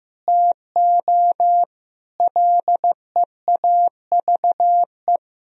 Morse code, heard as a keyed tone, about 15 wpm, 700 Hz. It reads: TO LEAVE
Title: Sentences from Top 1000 Words - Review Part 1 - 15wpm